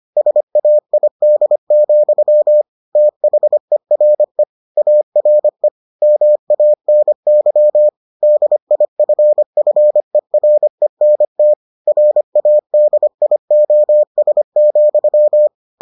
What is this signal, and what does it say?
Morse code, heard as a keyed tone, about 25 wpm, 600 Hz. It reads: SAID, THERE ARE MANY DIFFERENT RADIOS,